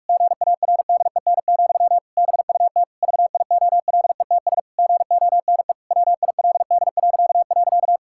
Morse code, heard as a keyed tone, about 41 wpm, 700 Hz. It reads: GARDEN, BUT VIOLETS GONE WILD..
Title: Pangram Challenge v2 - 45wpm